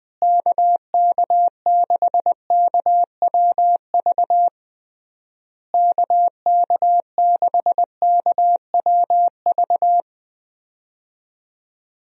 Morse code, heard as a keyed tone, about 21 wpm, 700 Hz. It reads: KK6KWV KK6KWV
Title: USA Call Signs - 6 Characters - Repeated 3x - 20wpm